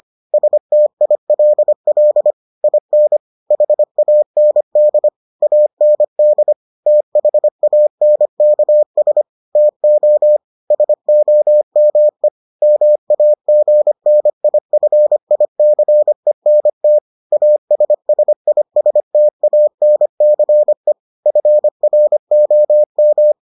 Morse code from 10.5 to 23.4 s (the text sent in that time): SOME MAGNIFICENT ASSISTANCE FROM